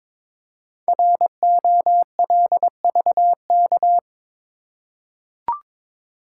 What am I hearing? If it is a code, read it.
Morse code, heard as a keyed tone, about 22 wpm, 690 Hz. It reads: ROLVK
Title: Code Groups - 5 Letter:Numbers - 22wpm